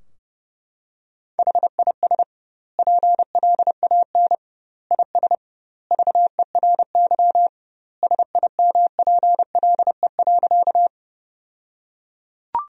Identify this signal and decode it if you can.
Morse code, heard as a keyed tone, about 30 wpm, 700 Hz. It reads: HIS PLAN IS VERY SIMPLE.
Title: 2x Word-Spacing - Sentences from Top 500 Words - 30wpm